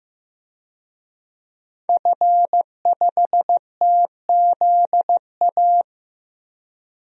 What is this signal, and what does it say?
Morse code, heard as a keyed tone, about 15 wpm, 700 Hz. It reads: F5TZA
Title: DXCC-100 Call Signs - 5 Characters - 15wpm